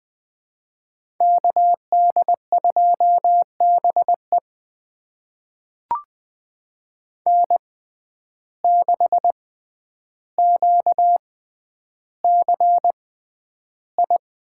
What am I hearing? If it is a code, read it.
Morse code, heard as a keyed tone, about 21 wpm, spaced as for 10 wpm, 700 Hz. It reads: KD2BE N6QCI
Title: USA Call Signs - 5 Characters - 6x Character Spacing - 20wpm